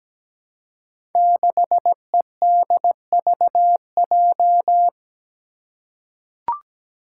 Morse code, heard as a keyed tone, about 17 wpm, 700 Hz. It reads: 6EDVJ
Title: Code Groups - 5 Letter:Numbers - 17wpm